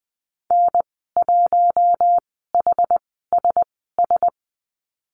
Morse code, heard as a keyed tone, about 20 wpm, spaced as for 15 wpm, 700 Hz. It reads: N1HSS
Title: 088 Lesson 15g - Number 5 - Call Signs 20wpm